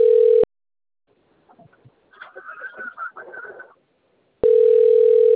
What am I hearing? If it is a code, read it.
Ringback tone (North America), 440 + 480 Hz.